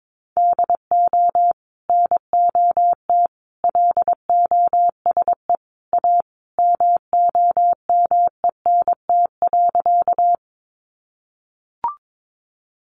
Morse code, heard as a keyed tone, about 22 wpm, 700 Hz. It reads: DO NOT LOSE A MOMENT.
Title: Sentences from Top 1100 Words 22wpm